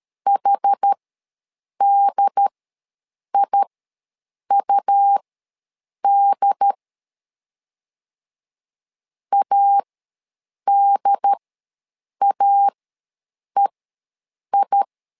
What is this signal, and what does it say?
Morse code, heard as a keyed tone, about 13 wpm, spaced as for 7 wpm, 800 Hz. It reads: HDIUD ADAEI
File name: LEZIONI / 2 UHDN / 2 UHD ws 12wpm 1722